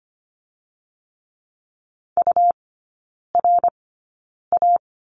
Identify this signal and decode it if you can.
Morse code, heard as a keyed tone, about 25 wpm, 700 Hz. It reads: U R A
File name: Lesson 17e - Letter u - Three Characters 25wpm